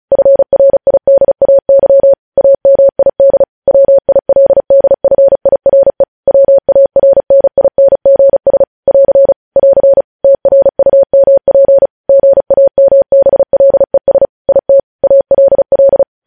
Morse code, heard as a keyed tone, about 36 wpm, 550 Hz. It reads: FRIDAY AMID WILDFIRE WARNINGS <AR> <AR> TRUMP GAMBLES IT ALL
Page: news headlines in morse code at 35 wpm